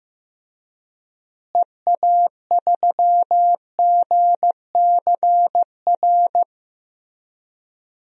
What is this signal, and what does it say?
Morse code, heard as a keyed tone, about 15 wpm, 700 Hz. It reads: EA3GCR
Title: DXCC-100 Call Signs - 6 Characters - Repeated 3x - 15wpm